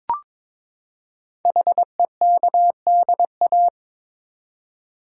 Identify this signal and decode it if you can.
Morse code, heard as a keyed tone, about 22 wpm, 700 Hz. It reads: HEKDA
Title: Code Groups - 5 Letters - 22wpm